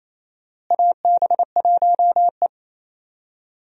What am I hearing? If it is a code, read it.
Morse code, heard as a keyed tone, about 28 wpm, 700 Hz. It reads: AB1E
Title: USA Call Signs - 4 Characters - Repeated 3x - 28wpm